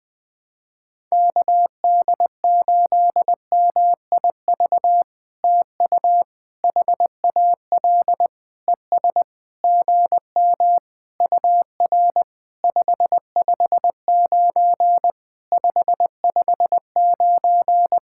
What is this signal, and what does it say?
Morse code, heard as a keyed tone, about 20 wpm, 700 Hz. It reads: KD8MIV TU HAL ES GM UR 559 559
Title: POTA QSO Practice - 20wpm